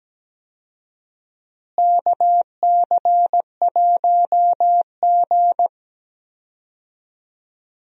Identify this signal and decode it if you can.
Morse code, heard as a keyed tone, about 17 wpm, 700 Hz. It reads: KC1G
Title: USA Call Signs - 4 Characters - Repeated 3x - 17wpm